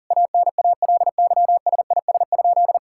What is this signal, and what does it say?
Morse code, heard as a keyed tone, about 37 wpm, 700 Hz. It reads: ANALYSIS?
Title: Sentences from Top 1000 Words - Review Part 5 - 40wpm